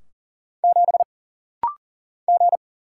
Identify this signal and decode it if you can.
Morse code, heard as a keyed tone, about 40 wpm, 700 Hz. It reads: 7 G
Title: Single Letter-Number - Rapid-Fire 40wpm